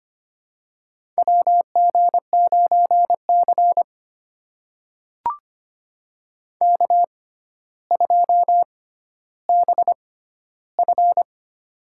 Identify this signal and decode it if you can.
Morse code, heard as a keyed tone, about 25 wpm, spaced as for 13 wpm, 700 Hz. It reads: WG9C K2BF
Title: USA Call Signs - 4 Characters - 6x Character Spacing - 25wpm